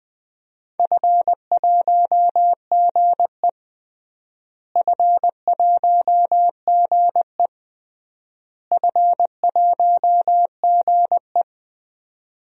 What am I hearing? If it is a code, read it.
Morse code, heard as a keyed tone, about 20 wpm, 700 Hz. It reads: F1GE F1GE F1GE
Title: DXCC-100 Call Signs - 4 Characters - Repeated 3x - 20wpm